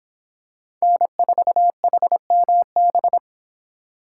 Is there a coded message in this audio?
Yes: Morse code from 0.8 to 3.2 s, heard as a keyed tone, about 25 wpm, 700 Hz. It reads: N4HMB